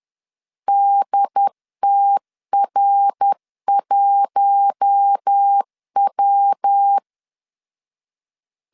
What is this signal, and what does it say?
Morse code, heard as a keyed tone, about 10 wpm, 800 Hz. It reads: DTR1W
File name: LEZIONI / 7 GF9Z / 7 GF9 10wpm 2747